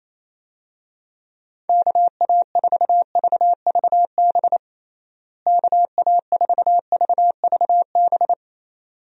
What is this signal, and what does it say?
Morse code, heard as a keyed tone, about 28 wpm, 700 Hz. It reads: KA4VVB KA4VVB
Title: USA Call Signs - 6 Characters - Repeated 2x - 28wpm